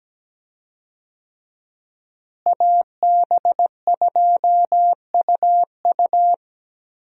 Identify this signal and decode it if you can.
Morse code, heard as a keyed tone, about 17 wpm, 700 Hz. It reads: AB2UU
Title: USA Call Signs - 5 Characters - Repeated 3x - 17wpm